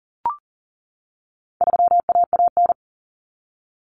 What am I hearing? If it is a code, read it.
Morse code, heard as a keyed tone, about 40 wpm, 700 Hz. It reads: 3AAN
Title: Common Cut Numbers - 4-Digit 40wpm